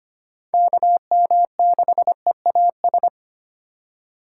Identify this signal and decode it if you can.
Morse code, heard as a keyed tone, about 25 wpm, 700 Hz. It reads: KM6EAS
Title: USA Call Signs - 6 Characters - 4x Character Spacing - 25wpm